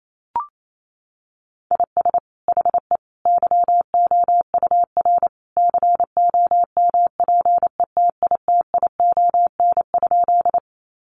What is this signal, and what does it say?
Morse code, heard as a keyed tone, about 28 wpm, 700 Hz. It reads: IS HE YOUR COMPETITION?
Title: Sentences from Top 1000 Words 28wpm